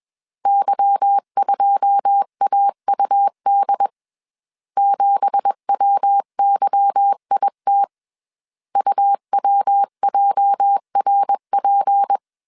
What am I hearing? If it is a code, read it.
Morse code, heard as a keyed tone, about 21 wpm, 800 Hz. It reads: Y2AUD 7WYIT UWJRP
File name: CW / DET / 1 9 1500C S C 20wpm 2310